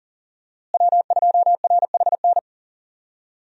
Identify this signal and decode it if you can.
Morse code, heard as a keyed tone, about 40 wpm, 700 Hz. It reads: W2RHN
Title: USA Call Signs - 5 Characters - Repeated 2x - 40wpm